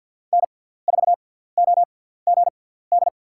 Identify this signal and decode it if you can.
Morse code, heard as a keyed tone, about 51 wpm, spaced as for 17 wpm, 700 Hz. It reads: N4OGD
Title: USA Call Signs - 5 Characters - 6x Character Spacing - 50wpm